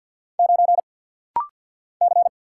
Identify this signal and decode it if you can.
Morse code, heard as a keyed tone, about 48 wpm, 700 Hz. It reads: 9 C
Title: Single Letter-Number - Rapid-Fire 50wpm